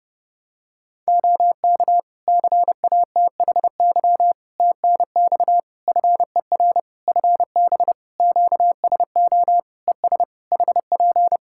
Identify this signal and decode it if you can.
Morse code, heard as a keyed tone, about 32 wpm, 700 Hz. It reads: OK CATHY TNX FER FB QSO ES HP
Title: K1USN SST QSO Practice 30wpm